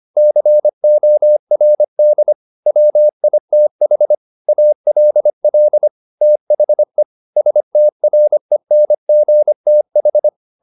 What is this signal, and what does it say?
Morse code, heard as a keyed tone, about 25 wpm, 600 Hz. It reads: CORD WITH ALL THE STRENGTH
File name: the old man and the sea part 1 25wpm